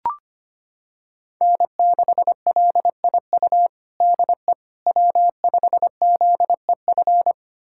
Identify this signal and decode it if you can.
Morse code, heard as a keyed tone, about 25 wpm, 700 Hz. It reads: N6LIU DE W5ZEF
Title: USA Callsign DE Callsign - 5 Characters - 25wpm